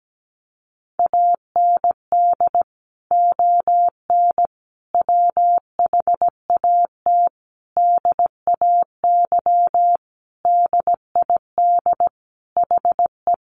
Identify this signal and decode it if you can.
Morse code, heard as a keyed tone, about 17 wpm, 700 Hz. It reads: AND ON WHAT DAY DID HE
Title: Sentences from Top 1100 Words 17wpm